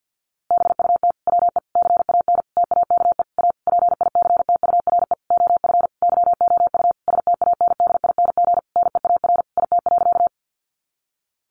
Morse code, heard as a keyed tone, about 50 wpm, 700 Hz. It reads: BUT WE CAN TAKE A PICTURE OF YOU STANDING NEAR IT.
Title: A Sound Of Thunder 50wpm